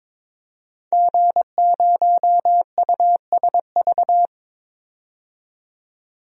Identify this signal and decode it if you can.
Morse code, heard as a keyed tone, about 22 wpm, 700 Hz. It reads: G0USV
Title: DXCC-100 Call Signs - 5 Characters - 22wpm